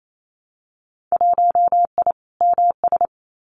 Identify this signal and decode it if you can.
Morse code, heard as a keyed tone, about 28 wpm, 700 Hz. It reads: 1I MS
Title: Winter Field Day Exchanges Encore 28wpm